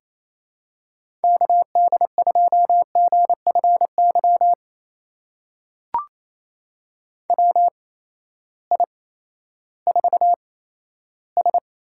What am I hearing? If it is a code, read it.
Morse code, heard as a keyed tone, about 27 wpm, spaced as for 14 wpm, 700 Hz. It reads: KD2GFY WI4S
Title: USA Call Signs - 8x Character Spacing - 28wpm